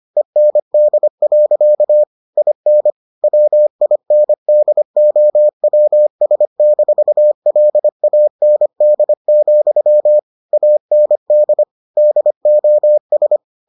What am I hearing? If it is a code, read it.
Morse code, heard as a keyed tone, about 25 wpm, 600 Hz. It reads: END. IN WINDOWS-LAND, AND DOS